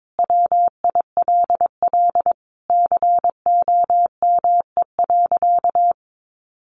Morse code, heard as a keyed tone, about 22 wpm, 700 Hz. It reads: WILL COME.
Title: Sentences from Top 1500 Words 22wpm